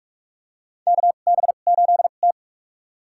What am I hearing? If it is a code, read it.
Morse code, heard as a keyed tone, about 44 wpm, spaced as for 32 wpm, 700 Hz. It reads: KB8T
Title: USA Call Signs - 2x Character Spacing - 45wpm